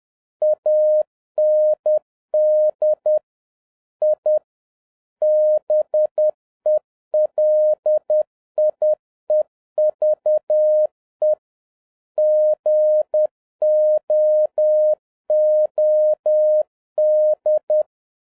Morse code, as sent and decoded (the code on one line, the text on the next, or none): .- -. -.. / .. / -... .-.. .. ...- . / --. --- --- -..
AND I BELIEVE GOOD